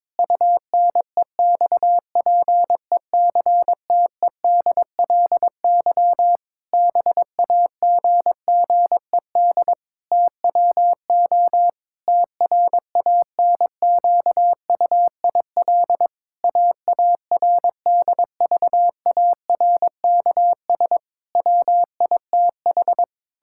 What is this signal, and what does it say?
Morse code, heard as a keyed tone, about 22 wpm, 700 Hz. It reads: UNEXPECTEDLY BAGGED TWO TRANQUIL AARDVARKS WITH
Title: Pangram Challenge v2 - 22wpm